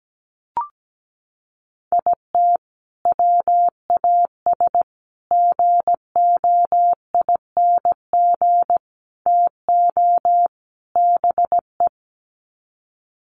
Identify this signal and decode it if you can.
Morse code, heard as a keyed tone, about 17 wpm, 700 Hz. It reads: IT WAS GOING TO BE
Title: Sets of 5 Words 17wpm